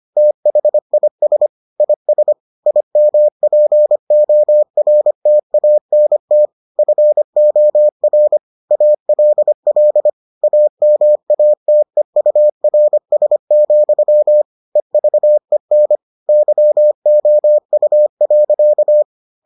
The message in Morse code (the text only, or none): THIS IS IMPORTANT FOR ALL AMATEURS, EVEN YOU.